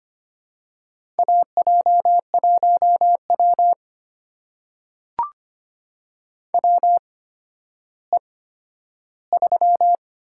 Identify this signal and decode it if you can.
Morse code, heard as a keyed tone, about 25 wpm, spaced as for 12 wpm, 700 Hz. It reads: AJ1W WE3
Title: USA Call Signs - 4 Characters - 8x Character Spacing - 25wpm